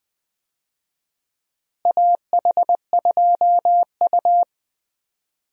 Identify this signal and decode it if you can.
Morse code, heard as a keyed tone, about 20 wpm, 700 Hz. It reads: AH2U
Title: USA Call Signs - 4 Characters - Repeated 3x - 20wpm